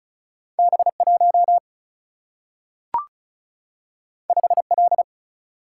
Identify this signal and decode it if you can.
Morse code, heard as a keyed tone, about 36 wpm, 700 Hz. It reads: B 1 5 L
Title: Single-Letter Single-Number 35wpm